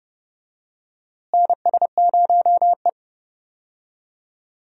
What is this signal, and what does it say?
Morse code, heard as a keyed tone, about 31 wpm, 700 Hz. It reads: NS0E
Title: USA Call Signs - 4 Characters - Repeated 2x - 30wpm